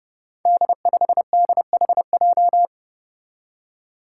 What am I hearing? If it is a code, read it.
Morse code, heard as a keyed tone, about 30 wpm, 700 Hz. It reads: D5DHJ
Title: Five Challenging Letter-Number Combinations 30wpm